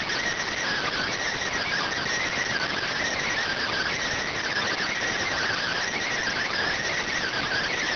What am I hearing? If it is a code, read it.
Emergency Alert System (SAME) tones (header bursts), heard as harsh buzzing bursts. It is heard from 0.7 to 7.3 s.